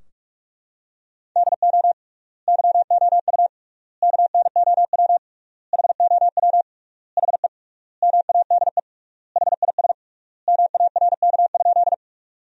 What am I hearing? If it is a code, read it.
Morse code, heard as a keyed tone, about 41 wpm, 700 Hz. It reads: DO YOU KNOW HOW HE MADE HIS MARK?
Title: 3x Word-Spacing - Sentences from Top 300 Words - 45wpm